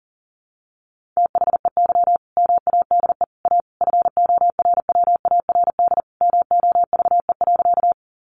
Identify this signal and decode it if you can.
Morse code, heard as a keyed tone, about 40 wpm, 700 Hz. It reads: THEY MADE A FORWARD MOVE.
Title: Sentences from Top 1300 Words 40wpm